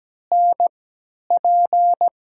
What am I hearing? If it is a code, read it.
Morse code, heard as a keyed tone, about 17 wpm, spaced as for 10 wpm, 700 Hz. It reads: NP
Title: USA Call Signs - 3x Character Spacing - 17wpm